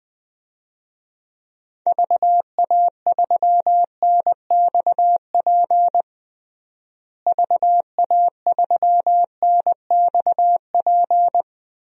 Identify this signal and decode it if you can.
Morse code, heard as a keyed tone, about 20 wpm, 700 Hz. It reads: VA3NXP VA3NXP
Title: DXCC-100 Call Signs - 6 Characters - Repeated 2x - 20wpm